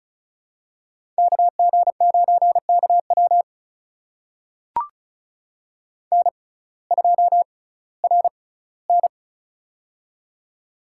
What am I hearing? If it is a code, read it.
Morse code, heard as a keyed tone, about 36 wpm, spaced as for 19 wpm, 700 Hz. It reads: KG9KW N2RN
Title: USA Call Signs - 6x Character Spacing - 35wpm